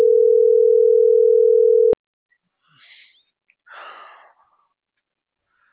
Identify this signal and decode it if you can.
Ringback tone (North America), 440 + 480 Hz.